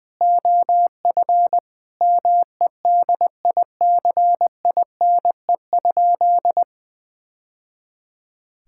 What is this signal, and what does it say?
Morse code, heard as a keyed tone, about 20 wpm, 700 Hz. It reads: OF MEDICINE?
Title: Sentences from Top 1000 Words - Review Part 6 - 20wpm